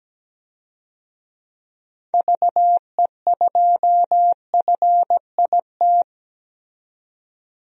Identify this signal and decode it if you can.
Morse code, heard as a keyed tone, about 17 wpm, 700 Hz. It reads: VE2FIT
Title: DXCC-100 Call Signs - 6 Characters - 17wpm